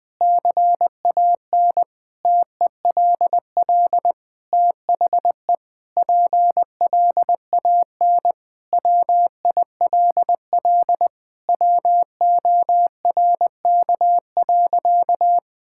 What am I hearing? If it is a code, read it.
Morse code, heard as a keyed tone, about 21 wpm, 700 Hz. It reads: CAN TELL THE PLAN WILL WORK.
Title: Sentences with 6-Word Phrases - 20wpm